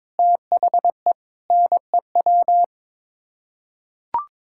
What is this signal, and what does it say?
Morse code, heard as a keyed tone, about 22 wpm, 700 Hz. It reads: THE NEW
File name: practice / 22wpm / Sets of 8 Words - 22wpm